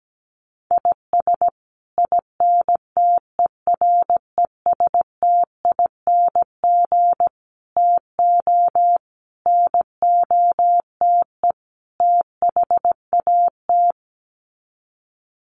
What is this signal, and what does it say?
Morse code, heard as a keyed tone, about 17 wpm, 700 Hz. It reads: IS INTERESTING TO NOTE THAT
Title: Sets of 5 Words Encore2 17wpm